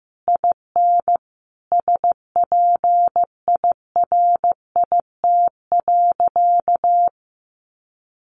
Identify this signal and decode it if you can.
Morse code, heard as a keyed tone, about 15 wpm, 700 Hz. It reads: IN SPIRIT.
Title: Sentences from Top 1400 Words 15wpm